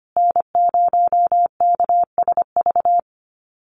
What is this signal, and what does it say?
Morse code, heard as a keyed tone, about 25 wpm, 700 Hz. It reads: N0KSV
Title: USA Call Signs Repeated 2x 25wpm